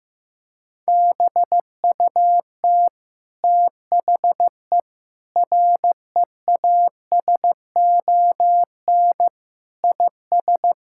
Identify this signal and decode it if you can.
Morse code, heard as a keyed tone, about 15 wpm, 700 Hz. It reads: BUT THE REASON IS
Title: Sentences from Top 1600 Words 15wpm